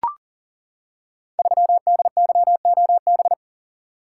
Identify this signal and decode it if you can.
Morse code, heard as a keyed tone, about 39 wpm, 700 Hz. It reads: 3DYOB
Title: Code Groups - 5 Letter:Numbers - 40wpm